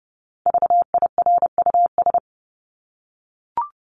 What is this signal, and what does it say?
Morse code, heard as a keyed tone, about 30 wpm, 700 Hz. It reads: VIRUS
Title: Top 1251-1500 Words 30wpm